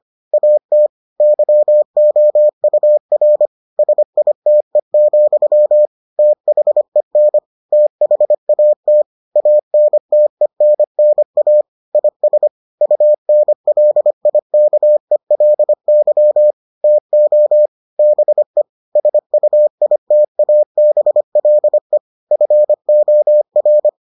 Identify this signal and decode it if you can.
Morse code, heard as a keyed tone, about 25 wpm, 600 Hz. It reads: AT YOUR SITE, THEN THAT ANTENNA IS UNLIKELY TO BE SUITABLE FOR